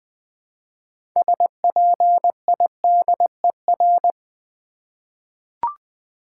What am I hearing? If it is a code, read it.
Morse code, heard as a keyed tone, about 20 wpm, 700 Hz. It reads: SPIDER
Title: 6 Letter Words - First 3 Letters Given - 20wpm